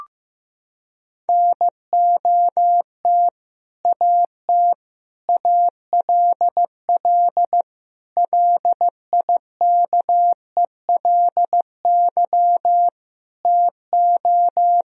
Morse code, heard as a keyed tone, about 16 wpm, 700 Hz. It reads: NOT AT ALL LIKELY TO
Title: Sentences from Top 1000 Words - Review Part 4 - 15wpm